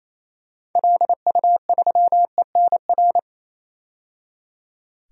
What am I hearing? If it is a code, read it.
Morse code, heard as a keyed tone, about 28 wpm, 700 Hz. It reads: LU3ENR
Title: DXCC-100 Call Signs - 6 Characters - 28wpm